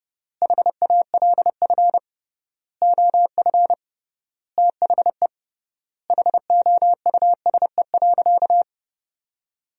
Morse code, heard as a keyed tone, about 30 wpm, 700 Hz. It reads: HALF OF THE HOUSE.
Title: 3x Word-Spacing - Sentences from Top 400 Words - 30wpm